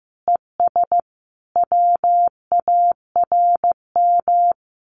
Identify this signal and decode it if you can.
Morse code, heard as a keyed tone, about 15 wpm, 700 Hz. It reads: ES WARM
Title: Common QSO Phrases Encore 15wpm